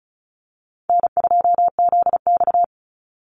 Morse code, heard as a keyed tone, about 34 wpm, 700 Hz. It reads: N2ZX
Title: Call Signs Repeated 2x - 35wpm